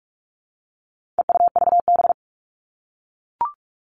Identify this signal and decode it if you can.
Morse code, heard as a keyed tone, about 44 wpm, 700 Hz. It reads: EUVB
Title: All Cut Numbers - 4-Digit 45wpm